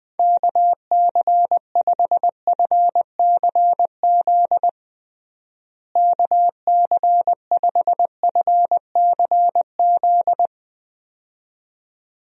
Morse code, heard as a keyed tone, about 21 wpm, 700 Hz. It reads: KC5FCZ KC5FCZ
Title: USA Call Signs - 6 Characters - Repeated 2x - 20wpm